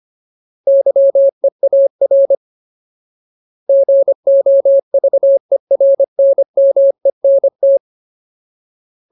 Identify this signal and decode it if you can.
Morse code, heard as a keyed tone, about 25 wpm, 550 Hz. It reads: YEAR GOVERNMENT
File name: Ultimate Word List / 25WPM Ultimate